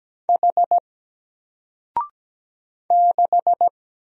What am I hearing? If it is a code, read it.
Morse code, heard as a keyed tone, about 17 wpm, 700 Hz. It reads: H 6
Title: Single Letter-Number Alt 17wpm